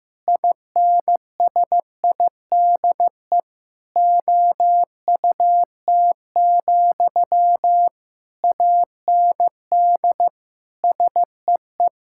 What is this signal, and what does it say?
Morse code, heard as a keyed tone, about 15 wpm, 700 Hz. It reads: INSIDE OUT, AND SEE
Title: Sentences from Top 1000 Words - Review Part 6 - 15wpm